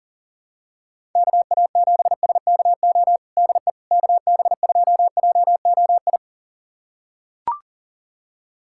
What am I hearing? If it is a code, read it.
Morse code, heard as a keyed tone, about 40 wpm, 700 Hz. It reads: KA7SKO DE KB2JOI
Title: USA Callsign DE Callsign - 6 Characters - 40wpm